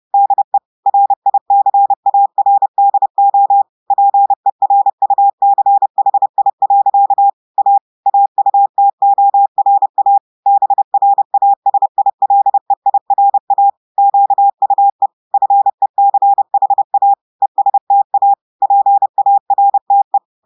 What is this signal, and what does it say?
Morse code, heard as a keyed tone, about 30 wpm, 800 Hz. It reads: DE RICARDO PERUCHI. A AUTORA BRASILEIRA QUE FECHA ESTA PARTE
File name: audios / portugais / portugais texte en clair 1 vitesse 30 mots minutes